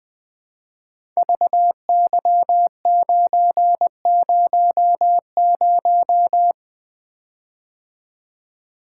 Morse code, heard as a keyed tone, about 20 wpm, 700 Hz. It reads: VY900
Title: Code Groups - 5 Letter:Numbers - 20wpm